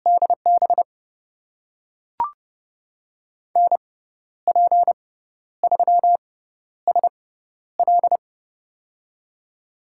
Morse code, heard as a keyed tone, about 30 wpm, spaced as for 12 wpm, 700 Hz. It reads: DB NP3SL
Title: USA Call Signs - 5 Characters - 6x Character Spacing - 30wpm